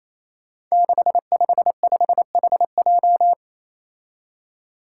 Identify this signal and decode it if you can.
Morse code, heard as a keyed tone, about 28 wpm, 700 Hz. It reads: 655HJ
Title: Five Challenging Letter-Number Combinations 28wpm